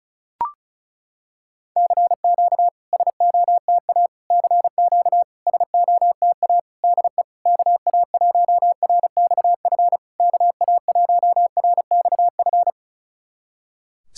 Morse code, heard as a keyed tone, about 37 wpm, 700 Hz. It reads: CQ SOTA CQ SOTA DE KA1RXF KA1RXF
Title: SOTA QSO Practice 35wpm